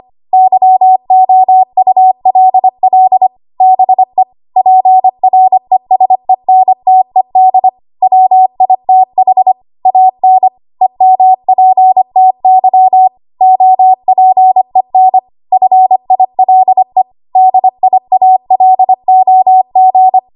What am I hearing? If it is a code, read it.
Morse code, heard as a keyed tone, about 25 wpm, 750 Hz. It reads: YOULL BE PRESENTED WITH AN EMPTY OPEN FILE DIALOG
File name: Archive / 25 WPM / 25WPM